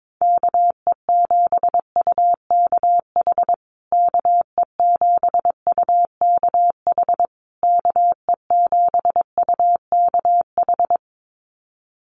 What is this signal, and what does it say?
Morse code, heard as a keyed tone, about 22 wpm, 700 Hz. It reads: KE7UKH KE7UKH KE7UKH
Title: USA Call Signs Repeated 3x 22wpm